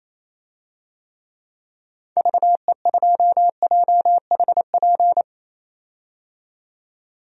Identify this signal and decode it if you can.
Morse code, heard as a keyed tone, about 27 wpm, 700 Hz. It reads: VE2JHP
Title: DXCC-100 Call Signs - 6 Characters - 28wpm